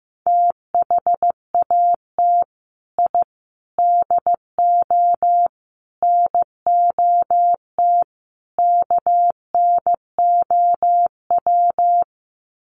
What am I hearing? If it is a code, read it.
Morse code, heard as a keyed tone, about 15 wpm, 700 Hz. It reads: THAT I DO NOT KNOW
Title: Sets of 5 Words Encore2 15wpm